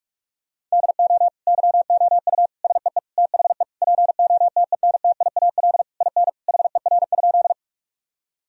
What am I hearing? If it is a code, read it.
Morse code, heard as a keyed tone, about 46 wpm, 700 Hz. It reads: DO YOU SEE THE POTENTIAL IN HER?
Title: Sentences from Top 1000 Words - Review Part 4 - 45wpm